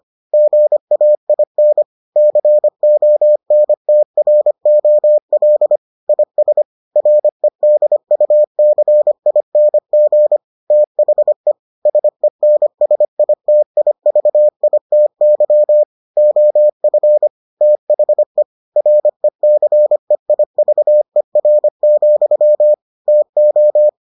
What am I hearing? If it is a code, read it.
Morse code, heard as a keyed tone, about 25 wpm, 600 Hz. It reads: GAIN CONTROL IS REDUCING THE SENSITIVITY OF THE RECEIVER, TO